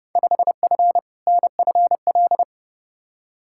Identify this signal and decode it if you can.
Morse code, heard as a keyed tone, about 32 wpm, 700 Hz. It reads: 5F NFL
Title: Field Day Exchanges Encore - 30wpm